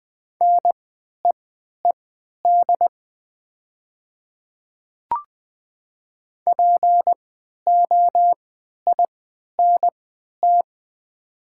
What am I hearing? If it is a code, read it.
Morse code, heard as a keyed tone, about 20 wpm, spaced as for 11 wpm, 700 Hz. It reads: NEED POINT
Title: 3x Character Spacing - Top 200 Words - 20wpm